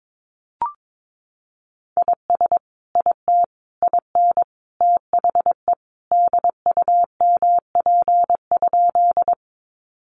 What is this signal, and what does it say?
Morse code, heard as a keyed tone, about 22 wpm, 700 Hz. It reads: IS IT IN THE DUMP?
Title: Sentences from Top 1300 Words 22wpm